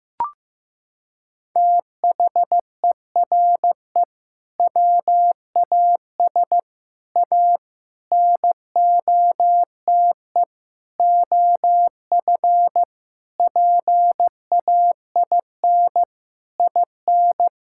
Morse code, heard as a keyed tone, about 15 wpm, 700 Hz. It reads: THERE WAS A NOTE OF PAIN IN
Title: Sentences from Top 1600 Words 15wpm